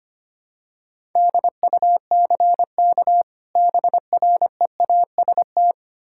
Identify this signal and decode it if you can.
Morse code, heard as a keyed tone, about 25 wpm, 700 Hz. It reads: DUCK BREAST
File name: practice / 25wpm / Difficult Words with Hints - 25wpm